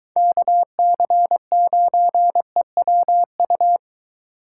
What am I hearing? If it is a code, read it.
Morse code, heard as a keyed tone, about 23 wpm, 700 Hz. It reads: KC9EWU